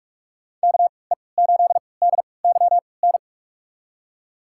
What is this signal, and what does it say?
Morse code, heard as a keyed tone, about 43 wpm, spaced as for 25 wpm, 690 Hz. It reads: KE8DYN
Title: USA Call Signs - 6 Characters - 3x Character Spacing - 45wpm